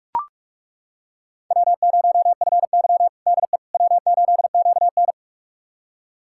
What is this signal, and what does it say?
Morse code, heard as a keyed tone, about 42 wpm, 700 Hz. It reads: W0FY DE W8QN
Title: USA Callsign DE Callsign - 4 Characters - 45wpm